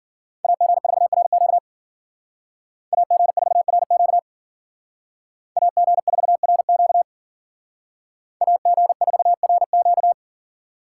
Morse code, heard as a keyed tone, about 52 wpm, 700 Hz. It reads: AG4RQ AG4RQ AG4RQ AG4RQ